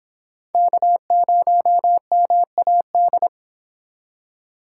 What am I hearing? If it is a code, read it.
Morse code, heard as a keyed tone, about 26 wpm, 700 Hz. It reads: K0MAD